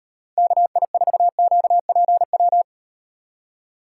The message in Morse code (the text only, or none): KI4QPW